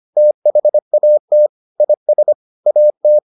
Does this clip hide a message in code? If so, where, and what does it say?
Yes: Morse code from 0.2 to 3.2 s, heard as a keyed tone, about 25 wpm, 600 Hz. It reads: THAT IS AT